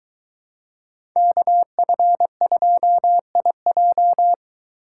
Morse code, heard as a keyed tone, about 23 wpm, 700 Hz. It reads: KF2IJ